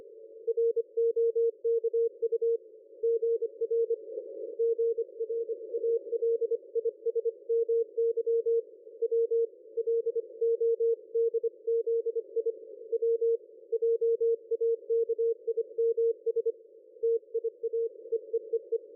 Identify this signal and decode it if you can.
Morse code, heard as a keyed tone, about 25 wpm, 460 Hz. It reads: ROKU GR GRALISMY W LODZI W JAKIMS TIA EEEE
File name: dlazaawansowanych / nagraniazpasma / ExpertSDR2 line out 23.01.2020 14.12.00